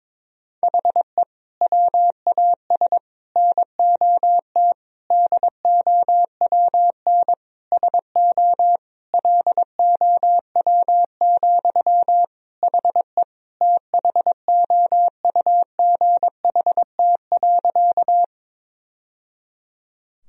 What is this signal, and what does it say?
Morse code, heard as a keyed tone, about 22 wpm, 700 Hz. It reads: HE WAS NOT DOWN SO LOW, HE THOUGHT.
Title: Sentences from Top 1000 Words - Review Part 1 - 22wpm